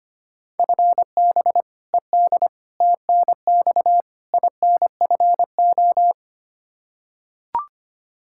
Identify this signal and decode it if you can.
Morse code, heard as a keyed tone, about 25 wpm, 700 Hz. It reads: FB ED TNX INFO
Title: K1USN SST QSO Practice 25wpm